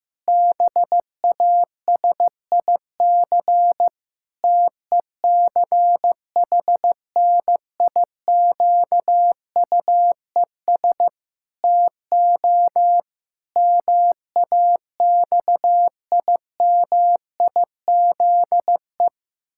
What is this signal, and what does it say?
Morse code, heard as a keyed tone, about 16 wpm, 700 Hz. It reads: BASIC TECHNIQUES TO MAXIMIZE
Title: Pangram Challenge v2 - 15wpm